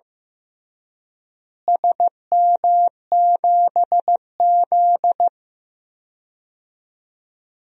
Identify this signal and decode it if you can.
Morse code, heard as a keyed tone, about 15 wpm, 700 Hz. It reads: SM7Z
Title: DXCC-100 Call Signs - 4 Characters - Repeated 2x - 15wpm